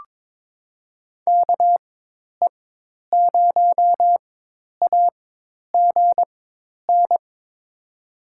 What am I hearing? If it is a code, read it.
Morse code, heard as a keyed tone, about 22 wpm, spaced as for 10 wpm, 700 Hz. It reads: KE0AGN
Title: USA Call Signs - 4x Character Spacing - 22wpm